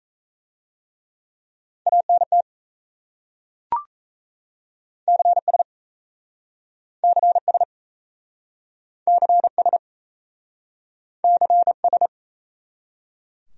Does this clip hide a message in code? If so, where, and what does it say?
Yes: Morse code from 1.9 to 12.1 s, heard as a keyed tone, about 38 wpm, 700 Hz. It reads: ANT CS CS CS CS